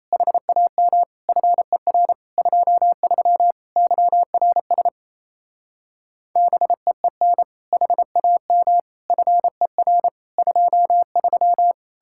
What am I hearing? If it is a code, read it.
Morse code, heard as a keyed tone, about 34 wpm, 700 Hz. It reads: HAM FER 23 YRS BEEN HAM FER 23
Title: Speed-Racing Common QSO Phrases - 28wpm